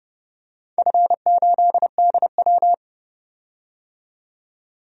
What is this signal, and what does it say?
Morse code, heard as a keyed tone, about 30 wpm, 700 Hz. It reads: F8DW
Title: DXCC-100 Call Signs - 4 Characters - 30wpm